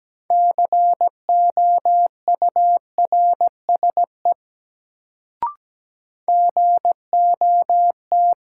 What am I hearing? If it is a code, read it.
Morse code, heard as a keyed tone, about 17 wpm, 700 Hz. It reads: COURSE GOT
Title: Sets of 2 Words - New Word Drill - 17wpm